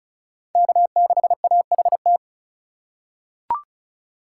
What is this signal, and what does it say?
Morse code, heard as a keyed tone, about 36 wpm, 700 Hz. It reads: K6AHT
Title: USA Call Signs - 5 Characters - 4x Character Spacing - 35wpm